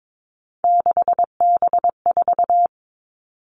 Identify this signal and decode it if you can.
Morse code, heard as a keyed tone, about 22 wpm, 700 Hz. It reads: 6B4
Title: All Cut Numbers - 3-Digit 22wpm